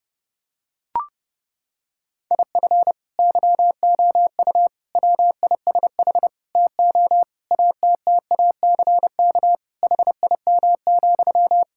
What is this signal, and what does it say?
Morse code, heard as a keyed tone, about 32 wpm, 700 Hz. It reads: IF YOU WISH TO ATTACK HIM,
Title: Sentences from Top 1600 Words 30wpm